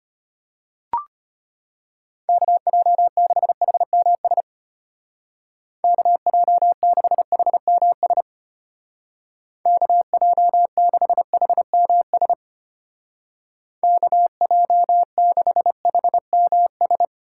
Morse code, heard as a keyed tone, about 31 wpm, 700 Hz. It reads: KJ6HMS KJ6HMS KJ6HMS KJ6HMS
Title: Speed Racing - USA 2x3 Call Signs 25wpm